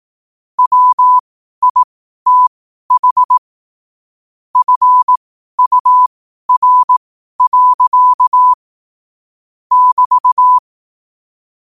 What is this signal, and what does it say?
Morse code, heard as a keyed tone, about 18 wpm, spaced as for 12 wpm, 1000 Hz. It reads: WITH FUR. <BT>
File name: qotdfiles / 12wpm